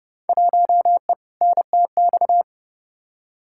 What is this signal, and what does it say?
Morse code, heard as a keyed tone, about 30 wpm, 700 Hz. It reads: 1E NTX
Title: Field Day Exchanges - 30wpm